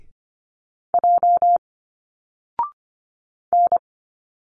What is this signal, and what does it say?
Morse code, heard as a keyed tone, about 25 wpm, 700 Hz. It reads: J N
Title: Single Letters 25wpm